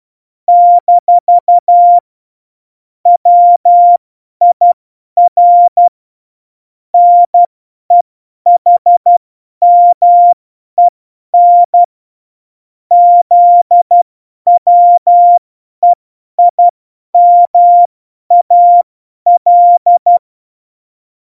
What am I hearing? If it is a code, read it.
Morse code, heard as a keyed tone, about 12 wpm, spaced as for 10 wpm, 700 Hz. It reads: - WIR NEHMEN ZWEIMAL
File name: Morsekurs / Witze / Witz 20. November 2018, 50 F70